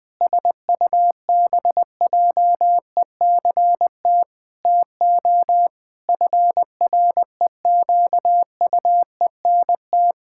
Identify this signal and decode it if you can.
Morse code, heard as a keyed tone, about 20 wpm, 700 Hz. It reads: SUBJECT TO FREQUENT
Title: Pangram Challenge v2 - 20wpm